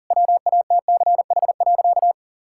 Morse code, heard as a keyed tone, about 40 wpm, 700 Hz. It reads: WATCH.
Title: Sentences from Top 1600 Words 40wpm